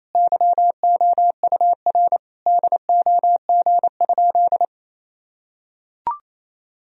Morse code, heard as a keyed tone, about 28 wpm, 700 Hz. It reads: YOUR DOG?
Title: Sentences from Top 1000 Words - Review Part 2 - 28wpm